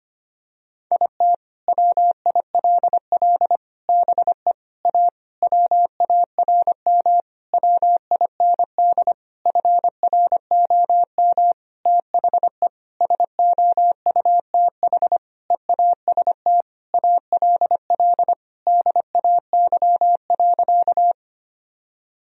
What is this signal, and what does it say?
Morse code, heard as a keyed tone, about 25 wpm, 700 Hz. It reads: IT WILL BE A WARM WIND FROM THE SOUTH EAST ALL DAY.
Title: Sentences from Top 1000 Words - Review Part 3 - 25wpm